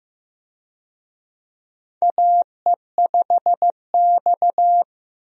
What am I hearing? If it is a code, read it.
Morse code, heard as a keyed tone, about 15 wpm, 700 Hz. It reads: AE5X
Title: USA Call Signs - 4 Characters - Repeated 3x - 15wpm